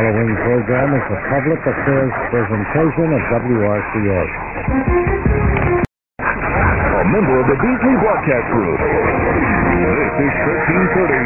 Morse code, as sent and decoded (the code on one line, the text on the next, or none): none